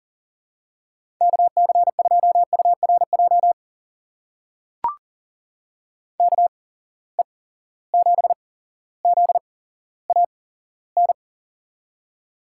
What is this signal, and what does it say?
Morse code, heard as a keyed tone, about 40 wpm, spaced as for 17 wpm, 700 Hz. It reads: KC2URJ KE7ZAN
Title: USA Call Signs - 6 Characters - 8x Character Spacing - 40wpm